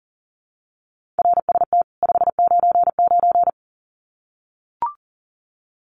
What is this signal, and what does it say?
Morse code, heard as a keyed tone, about 40 wpm, 700 Hz. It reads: RST 599
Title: Common QSO Phrases 40wpm